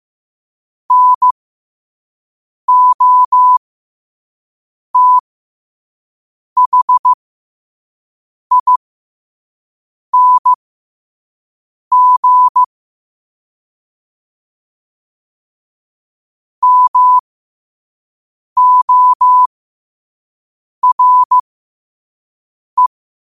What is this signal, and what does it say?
Morse code, heard as a keyed tone, about 15 wpm, spaced as for 5 wpm, 1000 Hz. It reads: NOTHING MORE
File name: qotdfiles / 5wpm